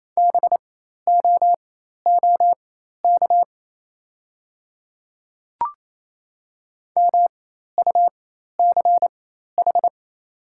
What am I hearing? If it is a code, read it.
Morse code, heard as a keyed tone, about 28 wpm, spaced as for 13 wpm, 700 Hz. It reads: BOOK MUCH
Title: 4x Character Spacing - Top 300 Words - 28wpm